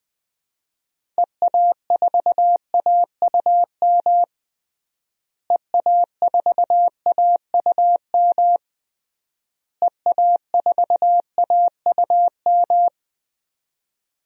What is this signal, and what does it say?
Morse code, heard as a keyed tone, about 21 wpm, 700 Hz. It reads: EA4AUM EA4AUM EA4AUM
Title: DXCC-100 Call Signs - 6 Characters - Repeated 3x - 20wpm